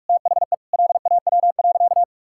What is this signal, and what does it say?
Morse code, heard as a keyed tone, about 46 wpm, 700 Hz. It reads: THE LAW.
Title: Sentences from Top 1000 Words - Review Part 5 - 45wpm